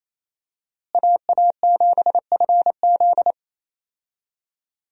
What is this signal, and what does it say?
Morse code, heard as a keyed tone, about 28 wpm, 700 Hz. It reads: AA7FZ
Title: USA Call Signs - 5 Characters - Repeated 2x - 28wpm